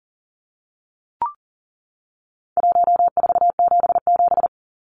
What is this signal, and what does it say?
Morse code, heard as a keyed tone, about 40 wpm, 700 Hz. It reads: 1477
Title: Numbers - Contest Style - 4-Digit 40wpm